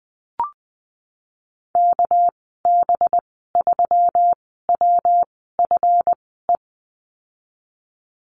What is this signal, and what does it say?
Morse code, heard as a keyed tone, about 20 wpm, spaced as for 15 wpm, 700 Hz. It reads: KB3WFE
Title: 200 Lesson 32g - Letter k - Call Signs 20wpm